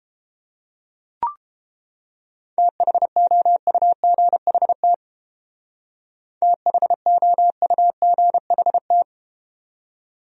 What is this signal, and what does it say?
Morse code, heard as a keyed tone, about 32 wpm, 700 Hz. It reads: THOUGHT THOUGHT